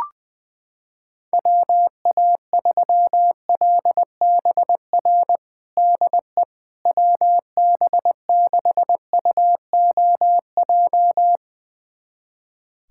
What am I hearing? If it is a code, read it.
Morse code, heard as a keyed tone, about 20 wpm, 700 Hz. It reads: WA3LBR DE WB6UOJ
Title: USA Callsign DE Callsign - 6 Characters - 20wpm